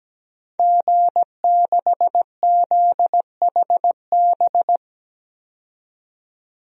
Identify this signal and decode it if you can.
Morse code, heard as a keyed tone, about 17 wpm, 700 Hz. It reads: G6ZHB
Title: DXCC-100 Call Signs - 5 Characters - 17wpm